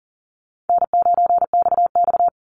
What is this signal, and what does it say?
Morse code, heard as a keyed tone, about 40 wpm, 700 Hz. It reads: N9XX
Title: Easy Call Signs Repeated 3x - 40wpm